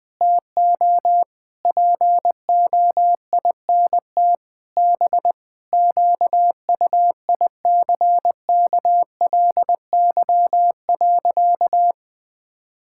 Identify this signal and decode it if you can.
Morse code, heard as a keyed tone, about 21 wpm, 700 Hz. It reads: TO POINT B QUICKLY.
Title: Sentences with 7-Word Phrases - 20wpm